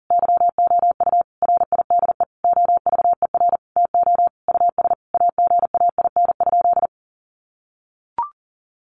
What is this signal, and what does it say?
Morse code, heard as a keyed tone, about 40 wpm, 700 Hz. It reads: YOU RIDE OVER TO US AGAIN?
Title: Sentences from Top 1200 Words 40wpm